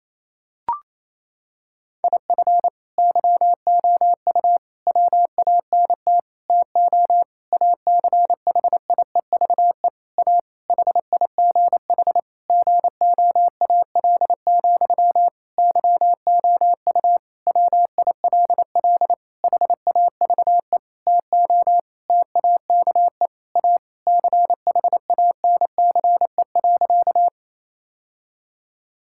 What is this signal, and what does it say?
Morse code, heard as a keyed tone, about 28 wpm, 700 Hz. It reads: IF YOU WANT TO ACHIEVE A HIGH GOAL, YOU WILL HAVE TO TAKE A CHANCE.
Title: Sentences from Top 1000 Words - Review Part 6 - 28wpm